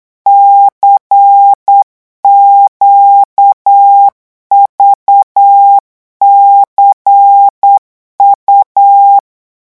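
Morse code, heard as a keyed tone, about 8 wpm, 800 Hz. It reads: CQVCU
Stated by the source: Morse code revision lesson at 8 words per minute, from a French amateur radio course.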